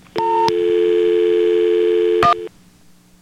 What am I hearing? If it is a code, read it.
Dial tone (North America), 350 + 440 Hz.